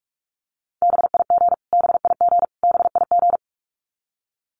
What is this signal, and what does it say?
Morse code, heard as a keyed tone, about 44 wpm, 700 Hz. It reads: BIG BIG BIG
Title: Top 500 Words Repeated 45wpm